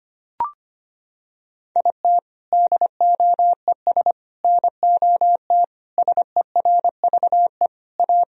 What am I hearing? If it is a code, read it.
Morse code, heard as a keyed tone, about 25 wpm, 700 Hz. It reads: IT DOES NOT SERVE A
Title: Sentences from Top 1000 Words - Review Part 2 - 25wpm